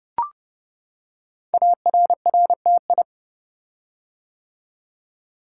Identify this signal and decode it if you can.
Morse code, heard as a keyed tone, about 30 wpm, 700 Hz. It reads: ARRTI
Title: Code Groups - 5 Letters - 30wpm